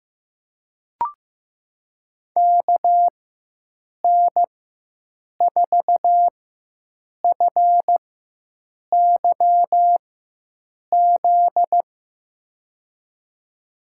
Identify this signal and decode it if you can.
Morse code, heard as a keyed tone, about 16 wpm, spaced as for 7 wpm, 700 Hz. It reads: KN4FYZ
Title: USA Call Signs - 6 Characters - 4x Character Spacing - 15wpm